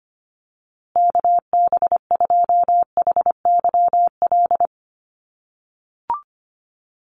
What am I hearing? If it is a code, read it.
Morse code, heard as a keyed tone, about 25 wpm, 700 Hz. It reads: KB2HYL
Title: USA Call Signs Repeated 2x 25wpm